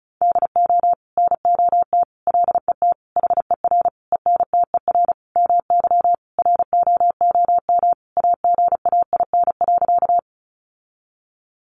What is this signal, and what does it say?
Morse code, heard as a keyed tone, about 34 wpm, 700 Hz. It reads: DO NOT LET HER ENTER MY ROOM AGAIN.
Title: Sentences from Top 1300 Words 35wpm